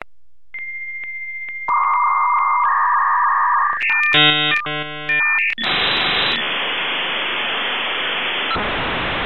Dial-up modem handshake.